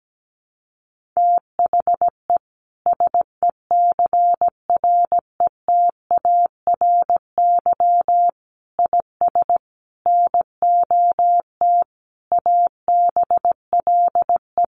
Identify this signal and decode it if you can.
Morse code, heard as a keyed tone, about 17 wpm, 700 Hz. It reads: THE SECRETARY IS NOT ABLE
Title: Sentences from Top 1300 Words 17wpm